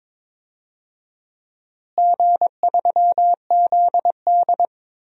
Morse code, heard as a keyed tone, about 21 wpm, 700 Hz. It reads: G3ZD
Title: DXCC-100 Call Signs - 4 Characters - Repeated 3x - 22wpm